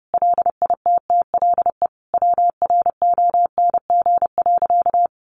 Morse code, heard as a keyed tone, about 30 wpm, 700 Hz. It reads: LITTLE WRONG.
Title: Sentences from Top 900 Words 30wpm